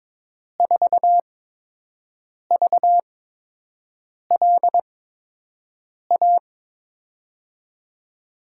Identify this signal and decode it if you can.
Morse code, heard as a keyed tone, about 22 wpm, spaced as for 6 wpm, 700 Hz. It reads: 4VLA
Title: USA Call Signs - 5 Characters - 8x Character Spacing - 22wpm